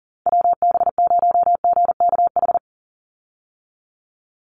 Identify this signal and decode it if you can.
Morse code, heard as a keyed tone, about 40 wpm, 700 Hz. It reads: WB0GKH
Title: Call Signs Repeated 3x - 40wpm